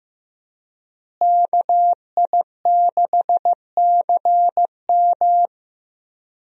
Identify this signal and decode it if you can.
Morse code, heard as a keyed tone, about 15 wpm, 700 Hz. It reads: KI6CM
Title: USA Call Signs - 5 Characters - Repeated 3x - 15wpm